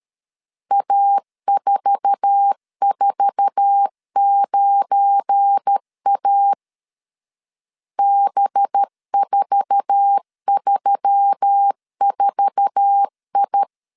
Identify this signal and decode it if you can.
Morse code, heard as a keyed tone, about 13 wpm, 800 Hz. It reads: A449A B434I
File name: LEZIONI / 8 YJ45 / 8 YJ4 12wpm 2151